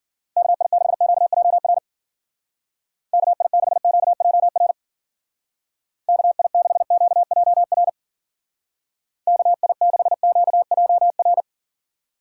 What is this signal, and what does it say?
Morse code, heard as a keyed tone, about 51 wpm, 700 Hz. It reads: KI6QJR KI6QJR KI6QJR KI6QJR